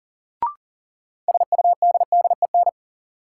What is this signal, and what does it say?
Morse code, heard as a keyed tone, about 40 wpm, 700 Hz. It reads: SUDDEN
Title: Sets of 4 Words - New Word Drill - 40wpm